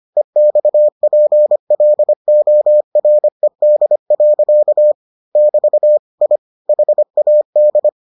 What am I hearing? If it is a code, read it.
Morse code, heard as a keyed tone, about 25 wpm, 600 Hz. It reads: EXPLORED. <BT> I HAD